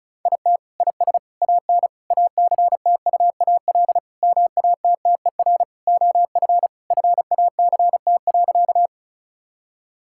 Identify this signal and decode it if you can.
Morse code, heard as a keyed tone, about 36 wpm, 700 Hz. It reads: IT IS AN ACTUAL MATTER OF FACT.
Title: Sentences from Top 1000 Words - Review Part 5 - 35wpm